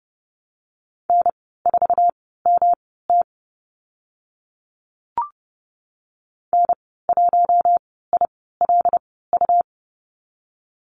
Morse code, heard as a keyed tone, about 30 wpm, spaced as for 17 wpm, 700 Hz. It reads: N4MT N1ILU
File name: Lesson 19g - Letter w - Call Signs 30wpm